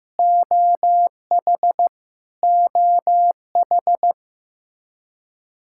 Morse code, heard as a keyed tone, about 15 wpm, 700 Hz. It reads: OH OH
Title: K1USN SST QSO Practice 15wpm